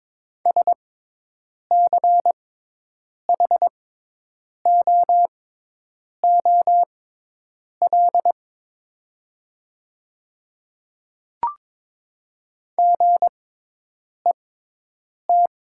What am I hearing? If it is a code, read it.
Morse code, heard as a keyed tone, about 22 wpm, spaced as for 8 wpm, 700 Hz. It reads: SCHOOL GET
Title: 6x Character Spacing - Top 300 Words - 22wpm